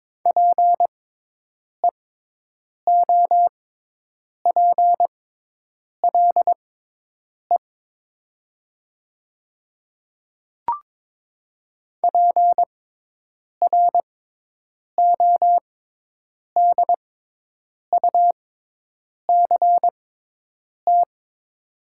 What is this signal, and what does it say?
Morse code, heard as a keyed tone, about 22 wpm, spaced as for 8 wpm, 700 Hz. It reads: PEOPLE PRODUCT
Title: 6x Character Spacing - Top 500 Words - 22wpm